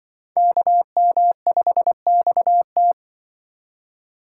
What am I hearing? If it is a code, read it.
Morse code, heard as a keyed tone, about 24 wpm, 700 Hz. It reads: KM5XT
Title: Speed Racing - USA 2x2 Call Signs 20wpm